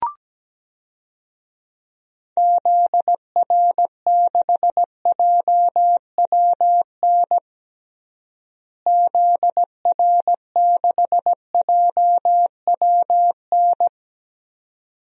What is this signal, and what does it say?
Morse code, heard as a keyed tone, about 17 wpm, 700 Hz. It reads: ZR6JWN ZR6JWN
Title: DXCC-100 Call Signs - 6 Characters - Repeated 2x - 17wpm